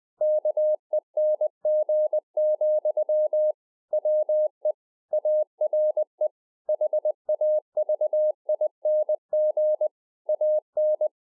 Morse code, heard as a keyed tone, about 20 wpm, 610 Hz. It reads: KENG, WE ARE HAVING AN